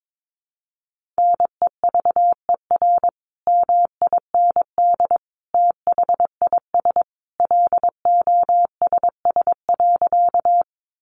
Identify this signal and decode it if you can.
Morse code, heard as a keyed tone, about 22 wpm, 700 Hz. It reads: NEVER MIND THIS LOSS.
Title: Sentences from Top 1400 Words 22wpm